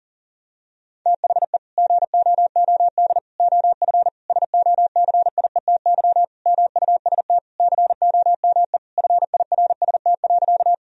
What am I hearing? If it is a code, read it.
Morse code, heard as a keyed tone, about 40 wpm, 700 Hz. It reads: THE GOOD OF SOCIETY MUST COME FIRST.
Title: Sentences from Top 1000 Words - Review Part 4 - 40wpm